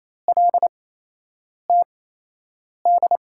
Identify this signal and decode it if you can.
Morse code, heard as a keyed tone, about 28 wpm, spaced as for 8 wpm, 700 Hz. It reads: LTD
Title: USA Call Signs - 5 Characters - 8x Character Spacing - 28wpm